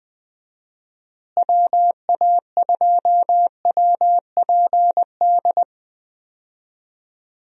Morse code, heard as a keyed tone, about 21 wpm, 700 Hz. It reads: WA2WPD
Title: USA Call Signs - 6 Characters - Repeated 2x - 20wpm